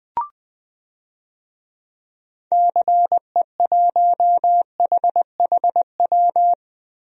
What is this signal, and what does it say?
Morse code, heard as a keyed tone, about 20 wpm, 700 Hz. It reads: CE1HHW
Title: DXCC-100 Call Signs - 6 Characters - Repeated 3x - 20wpm